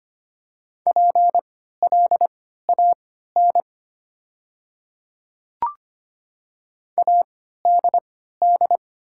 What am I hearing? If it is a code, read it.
Morse code, heard as a keyed tone, about 25 wpm, spaced as for 14 wpm, 700 Hz. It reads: PLAN ADD
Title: 3x Character Spacing - Top 500 Words - 25wpm